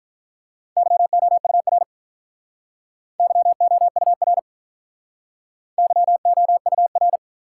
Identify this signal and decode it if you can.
Morse code, heard as a keyed tone, about 45 wpm, 700 Hz. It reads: YOUR YOUR YOUR